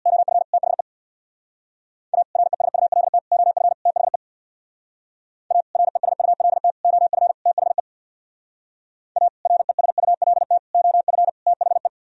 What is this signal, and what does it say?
Morse code, heard as a keyed tone, about 59 wpm, 700 Hz. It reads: OF THE A RESULT OF THE A RESULT OF THE A RESULT OF THE
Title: Speed-Racing - Sets of 4 Words 50wpm